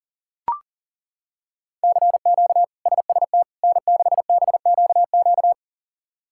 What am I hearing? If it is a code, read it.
Morse code, heard as a keyed tone, about 40 wpm, 700 Hz. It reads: CQ SST N6BQQ
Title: K1USN SST QSO Practice - 40wpm